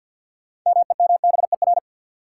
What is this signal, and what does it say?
Morse code, heard as a keyed tone, about 53 wpm, 700 Hz. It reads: MEMBER
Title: Sets of 3 Words - New Word Drill - 50wpm